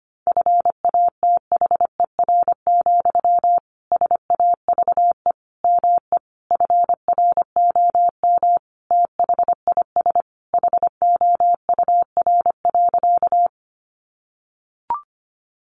Morse code, heard as a keyed tone, about 25 wpm, 700 Hz. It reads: FATHER, SAVE ME FROM THIS HOUR.
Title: Sentences from Top 1100 Words 25wpm